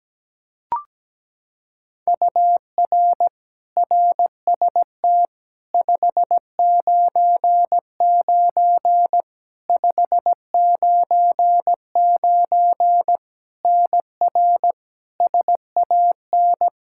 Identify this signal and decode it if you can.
Morse code, heard as a keyed tone, about 17 wpm, 700 Hz. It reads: UR RST 599 599 NR SAN
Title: K1USN SST QSO Practice 17wpm